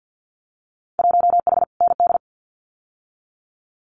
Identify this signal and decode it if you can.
Morse code, heard as a keyed tone, about 50 wpm, 700 Hz. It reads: 1H ND
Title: Winter Field Day Exchanges 50wpm